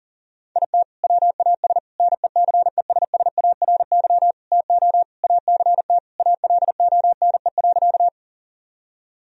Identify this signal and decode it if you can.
Morse code, heard as a keyed tone, about 40 wpm, 700 Hz. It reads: IT WAS NECESSARY TO ACT ALONE.
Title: Sentences from Top 1000 Words - Review Part 1 - 40wpm